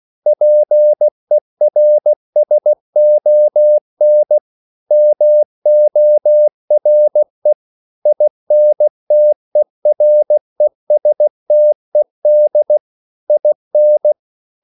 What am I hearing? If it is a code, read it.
Morse code, heard as a keyed tone, about 16 wpm, 600 Hz. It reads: PERSON MORE INTERESTED IN
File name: CW / Practice / EnglishQuotations0216